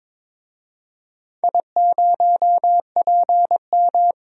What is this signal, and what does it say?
Morse code, heard as a keyed tone, about 22 wpm, 700 Hz. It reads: I0PM
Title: DXCC-100 Call Signs - 4 Characters - Repeated 3x - 22wpm